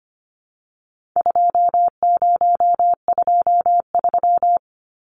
Morse code, heard as a keyed tone, about 25 wpm, 700 Hz. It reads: 2023
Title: Numbers - 4-Digit 25wpm